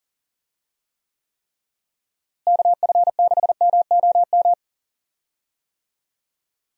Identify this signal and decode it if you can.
Morse code, heard as a keyed tone, about 40 wpm, 700 Hz. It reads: KF6MOM
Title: USA Call Signs - 6 Characters - Repeated 3x - 40wpm